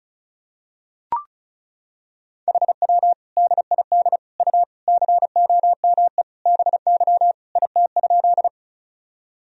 Morse code, heard as a keyed tone, about 36 wpm, 700 Hz. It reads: HW DID U COME BY IT?
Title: Sentences from Top 100 Words and Abbreviations 35wpm